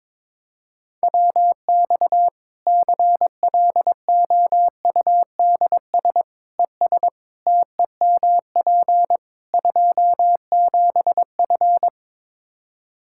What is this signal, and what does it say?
Morse code, heard as a keyed tone, about 22 wpm, 700 Hz. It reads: WX CLOUDS ES TEMP 27F
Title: Practice QSO - Exchange 2 - Info - 22wpm